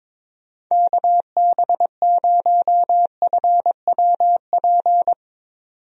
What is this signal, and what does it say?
Morse code, heard as a keyed tone, about 22 wpm, 700 Hz. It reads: KB0FWP
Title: USA Call Signs - 6 Characters - 22wpm